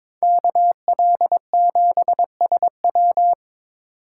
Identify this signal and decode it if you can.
Morse code, heard as a keyed tone, about 25 wpm, 700 Hz. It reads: KL7SW
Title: Speed Racing - USA 2x2 Call Signs 22wpm